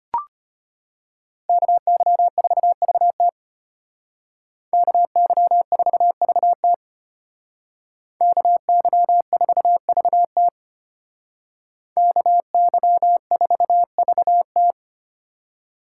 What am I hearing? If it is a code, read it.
Morse code, heard as a keyed tone, about 32 wpm, 700 Hz. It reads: KY4VT KY4VT KY4VT KY4VT